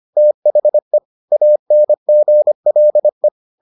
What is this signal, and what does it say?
Morse code, heard as a keyed tone, about 25 wpm, 600 Hz. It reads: THE ANGLE